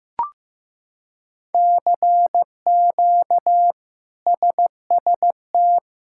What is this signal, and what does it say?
Morse code, heard as a keyed tone, about 15 wpm, 700 Hz. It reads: CQ SST
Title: K1USN SST QSO Practice - 15wpm